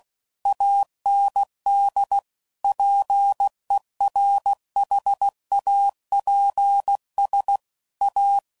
Morse code, heard as a keyed tone, about 16 wpm, 800 Hz. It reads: AND PERHAPS A
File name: CW Code Groups @15WPM/Technical Subjects